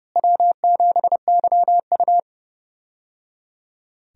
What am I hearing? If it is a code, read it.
Morse code, heard as a keyed tone, about 29 wpm, 700 Hz. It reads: W7YU
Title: USA Call Signs - 4 Characters - Repeated 3x - 30wpm